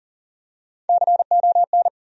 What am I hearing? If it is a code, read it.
Morse code, heard as a keyed tone, about 40 wpm, 700 Hz. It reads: CON
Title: USA Call Sign Postfixes - 3 Letters 40wpm